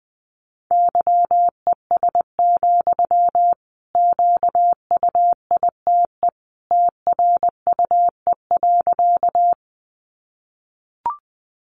Morse code, heard as a keyed tone, about 20 wpm, 700 Hz. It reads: YES, QUITE TRUE.
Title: Sentences from Top 700 Words 20wpm